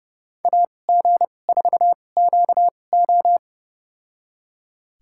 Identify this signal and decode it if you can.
Morse code, heard as a keyed tone, about 32 wpm, spaced as for 22 wpm, 700 Hz. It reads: AG4QO
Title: USA Call Signs - 2x Character Spacing - 30wpm